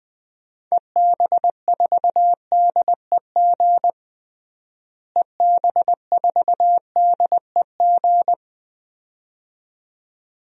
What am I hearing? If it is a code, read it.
Morse code, heard as a keyed tone, about 19 wpm, 700 Hz. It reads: EB4DEG EB4DEG
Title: DXCC-100 Call Signs - 6 Characters - Repeated 2x - 20wpm